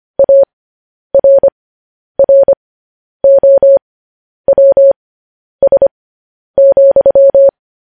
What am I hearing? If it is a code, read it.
Morse code, heard as a keyed tone, about 25 wpm, spaced as for 10 wpm, 550 Hz. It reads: ARROWS,
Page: news-headlines-in-morse-code-at-10-wpm